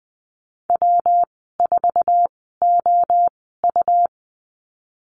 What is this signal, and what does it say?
Morse code, heard as a keyed tone, about 20 wpm, spaced as for 14 wpm, 700 Hz. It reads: W4OU
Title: 161 Lesson 26g - Letter g - Call Signs 20wpm